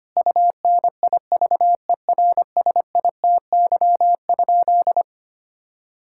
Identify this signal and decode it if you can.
Morse code, heard as a keyed tone, about 25 wpm, 700 Hz. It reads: UNIVERSITY?
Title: Sentences from Top 1000 Words - Review Part 5 - 25wpm